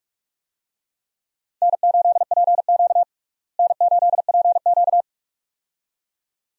Morse code, heard as a keyed tone, about 48 wpm, 700 Hz. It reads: N8PQ N8PQ
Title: USA Call Signs - 4 Characters - Repeated 2x - 45wpm